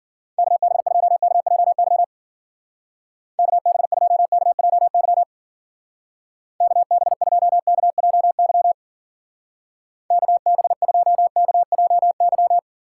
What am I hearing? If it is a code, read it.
Morse code, heard as a keyed tone, about 52 wpm, 700 Hz. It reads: KB2KJY KB2KJY KB2KJY KB2KJY